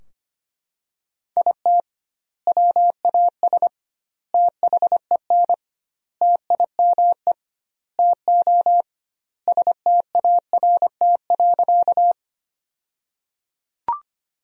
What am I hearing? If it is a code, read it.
Morse code, heard as a keyed tone, about 25 wpm, 700 Hz. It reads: IT WAS THEN TIME TO START.
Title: 2x Word-Spacing - Sentences from Top 300 Words - 25wpm